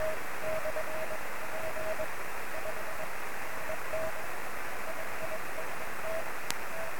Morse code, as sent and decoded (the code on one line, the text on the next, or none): none